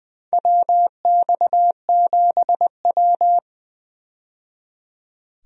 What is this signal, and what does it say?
Morse code, heard as a keyed tone, about 20 wpm, 700 Hz. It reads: WX7W
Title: USA Call Signs - 4 Characters - 20wpm